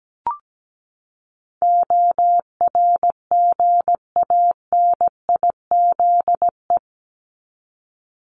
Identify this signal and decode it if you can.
Morse code, heard as a keyed tone, about 17 wpm, 700 Hz. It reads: ORGANIZE
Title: Top 1751-2000 Words 17wpm